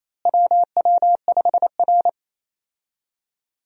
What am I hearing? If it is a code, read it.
Morse code, heard as a keyed tone, about 28 wpm, 700 Hz. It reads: WW5R
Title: USA Call Signs - 4 Characters - Repeated 2x - 28wpm